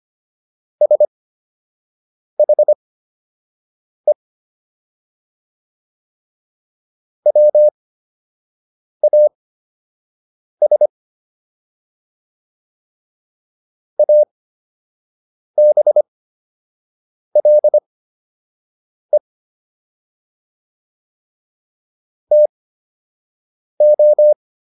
Morse code, heard as a keyed tone, about 25 wpm, spaced as for 6 wpm, 600 Hz. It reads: SHE WAS ABLE TO